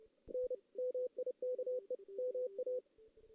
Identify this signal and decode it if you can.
Morse code, heard as a keyed tone, about 29 wpm, 500 Hz. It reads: N MIKE MA
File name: 2016 JAN CW NAQP / 2016 Jan naqp k5ot - 0-2